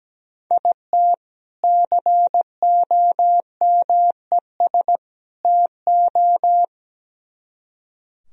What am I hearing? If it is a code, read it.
Morse code, heard as a keyed tone, about 19 wpm, 700 Hz. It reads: IT COMES TO
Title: Speed-Racing - Sets of 3 Words 17wpm